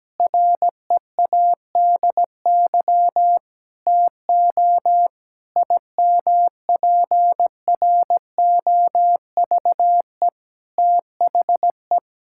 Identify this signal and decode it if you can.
Morse code, heard as a keyed tone, about 17 wpm, 700 Hz. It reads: READY TO IMPROVE THE
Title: Sentences from Top 1000 Words - Review Part 5 - 17wpm